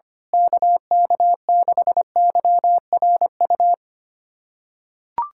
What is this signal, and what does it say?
Morse code, heard as a keyed tone, about 25 wpm, 700 Hz. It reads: KK6YRU
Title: USA Callsign DE Callsign - 6 Characters - 25wpm